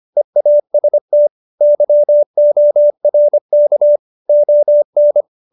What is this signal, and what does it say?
Morse code, heard as a keyed tone, about 25 wpm, 590 Hz. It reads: EAST YORK ON